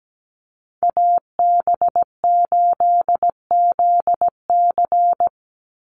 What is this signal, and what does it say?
Morse code, heard as a keyed tone, about 17 wpm, 700 Hz. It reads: AB8ZC
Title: USA Call Signs 17wpm